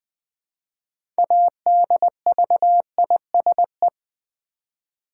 Morse code, heard as a keyed tone, about 20 wpm, 700 Hz. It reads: ADVISE
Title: Build A Word - 3 Letters At A Time - 6-Letter Words - 20wpm